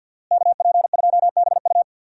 Morse code, heard as a keyed tone, about 50 wpm, 700 Hz. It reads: KP2BU
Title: USA Call Signs - 5 Characters - Repeated 3x - 50wpm